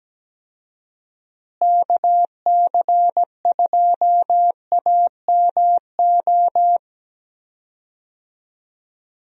Morse code, heard as a keyed tone, about 17 wpm, 700 Hz. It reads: KC2AMO
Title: USA Call Signs - 6 Characters - Repeated 2x - 17wpm